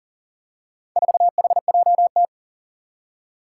Morse code, heard as a keyed tone, about 38 wpm, 700 Hz. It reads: 4HJT
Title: Four Challenging Letter-Number Combinations 40wpm